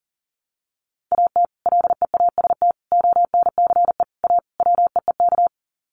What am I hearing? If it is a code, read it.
Morse code, heard as a keyed tone, about 40 wpm, 700 Hz. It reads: AT LEAST ONCE A WEEK
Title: Sets of 5 Words Encore2 40wpm